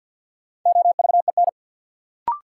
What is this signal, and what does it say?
Morse code, heard as a keyed tone, about 51 wpm, 700 Hz. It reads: OVEN
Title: Sentences from Top 1200 Words - New Word Drill Alt - 50wpm